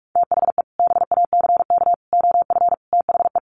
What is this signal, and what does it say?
Morse code, heard as a keyed tone, about 44 wpm, 700 Hz. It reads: THE BACK OF THE
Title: Sets of 5 Words Encore 45wpm